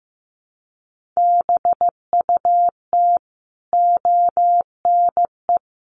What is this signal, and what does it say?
Morse code, heard as a keyed tone, about 15 wpm, 700 Hz. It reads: BUT ONE
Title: Sentences from Top 1100 Words 15wpm